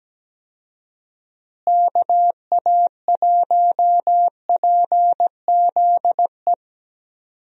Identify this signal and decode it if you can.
Morse code, heard as a keyed tone, about 17 wpm, 700 Hz. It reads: KA1PZE
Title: USA Call Signs - 6 Characters - Repeated 3x - 17wpm